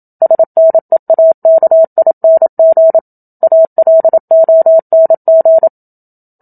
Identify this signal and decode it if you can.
Morse code, heard as a keyed tone, about 27 wpm, 650 Hz. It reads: SNEAKING ALONG
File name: Book Two / 01-Chapter 1 - 27 WPM